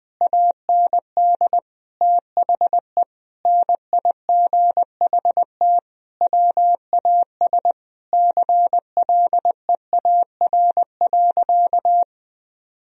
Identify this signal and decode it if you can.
Morse code, heard as a keyed tone, about 21 wpm, 700 Hz. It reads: AND THE NIGHT WAS CLEAR.
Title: Sentences from Top 1000 Words - Review Part 1 - 20wpm